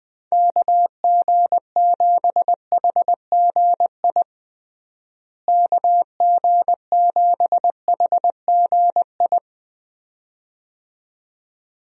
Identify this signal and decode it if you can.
Morse code, heard as a keyed tone, about 20 wpm, 700 Hz. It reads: KG7HGI KG7HGI
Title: USA Call Signs - 6 Characters - Repeated 2x - 20wpm